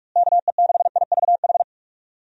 Morse code, heard as a keyed tone, about 41 wpm, 700 Hz. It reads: KE6IVH
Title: USA Call Signs - 6 Characters - Repeated 2x - 45wpm